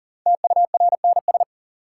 Morse code, heard as a keyed tone, about 40 wpm, 700 Hz. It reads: TURNS
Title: Sets of 5 Words Encore - New Word Drill - 40wpm